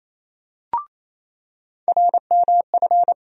Morse code, heard as a keyed tone, about 28 wpm, 700 Hz. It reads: RMF
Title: USA Call Sign Postfixes - 3 Letters 28wpm